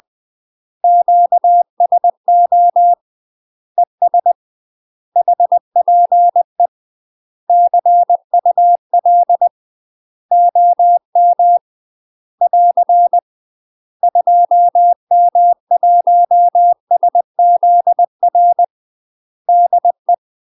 Morse code, heard as a keyed tone, about 20 wpm, 700 Hz. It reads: QSO ES HPE CUL OM <AR> 2M1SZR DE
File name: UK MASTER QSOs TRAINING / 20WPM / UK MASTER QSOs 0065